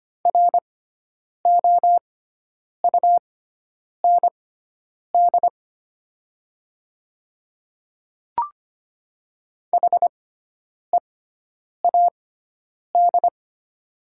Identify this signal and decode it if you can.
Morse code, heard as a keyed tone, about 25 wpm, spaced as for 9 wpm, 700 Hz. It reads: ROUND HEAD
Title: 6x Character Spacing - Top 300 Words - 25wpm